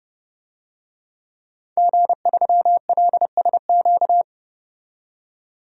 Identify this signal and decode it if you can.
Morse code, heard as a keyed tone, about 31 wpm, 700 Hz. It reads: G3LSQ
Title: DXCC-100 Call Signs - 5 Characters - 30wpm